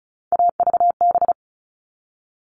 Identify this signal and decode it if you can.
Morse code, heard as a keyed tone, about 34 wpm, 700 Hz. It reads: AVB
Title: All Cut Numbers - 3-Digit 35wpm